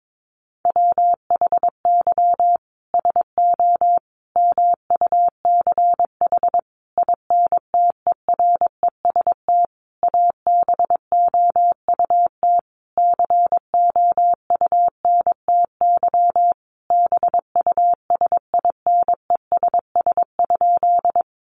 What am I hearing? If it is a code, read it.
Morse code, heard as a keyed tone, about 22 wpm, 700 Hz. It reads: WHY SO MUCH INTEREST ABOUT COUNTY BUSINESS?
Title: Sentences from Top 1200 Words 22wpm